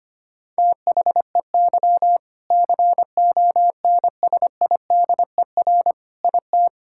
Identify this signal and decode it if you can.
Morse code, heard as a keyed tone, about 25 wpm, 700 Hz. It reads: THEY CONSIDER IT
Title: Sentences from Top 1000 Words - Review Part 5 - 25wpm